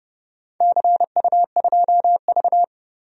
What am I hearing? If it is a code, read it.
Morse code, heard as a keyed tone, about 30 wpm, 700 Hz. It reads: CU2V
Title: DXCC-100 Call Signs - 4 Characters - 30wpm